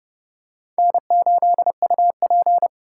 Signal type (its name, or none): Morse code